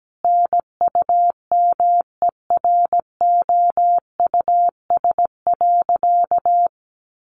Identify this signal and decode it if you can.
Morse code, heard as a keyed tone, about 17 wpm, 700 Hz. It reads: NUMEROUS.
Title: Sentences from Top 1400 Words 17wpm